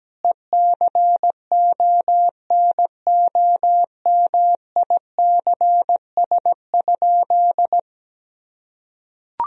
Morse code, heard as a keyed tone, about 17 wpm, 700 Hz. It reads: ECONOMICS?
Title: Sentences from Top 1000 Words - Review Part 3 - 17wpm